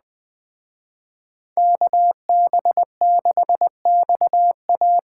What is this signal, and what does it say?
Morse code, heard as a keyed tone, about 20 wpm, 700 Hz. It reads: KB6XA
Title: Speed Racing - USA 2x2 Call Signs 17wpm